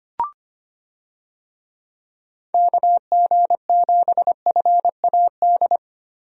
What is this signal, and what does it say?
Morse code, heard as a keyed tone, about 25 wpm, 700 Hz. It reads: KG7FAD
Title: USA Call Signs - 6 Characters - Repeated 2x - 25wpm